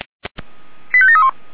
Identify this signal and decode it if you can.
Five-tone selective call (ZVEI), heard as quick runs of five tones, tones of about 70 ms. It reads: 85731